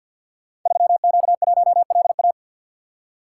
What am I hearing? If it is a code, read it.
Morse code, heard as a keyed tone, about 52 wpm, 700 Hz. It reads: 3Q1LA
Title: Code Groups - 5 Letter:Numbers - 50wpm